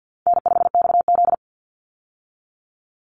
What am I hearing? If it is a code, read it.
Morse code, heard as a keyed tone, about 50 wpm, 700 Hz. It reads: N5XZ
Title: Easy Call Signs Repeated 2x - 50wpm